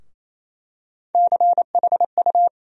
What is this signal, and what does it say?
Morse code, heard as a keyed tone, about 28 wpm, 700 Hz. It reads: CHU
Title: USA Call Sign Postfixes - 3 Letters 28wpm